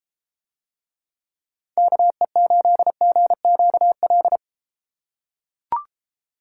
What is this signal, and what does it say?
Morse code, heard as a keyed tone, about 31 wpm, 700 Hz. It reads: KE8GQL